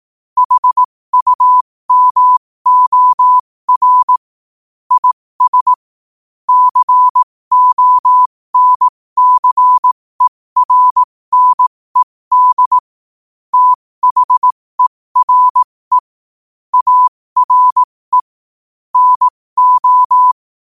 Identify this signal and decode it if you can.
Morse code, heard as a keyed tone, about 18 wpm, spaced as for 15 wpm, 1000 Hz. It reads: HUMOR IS CONCERNED THERE ARE NO
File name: qotdfiles / 15wpm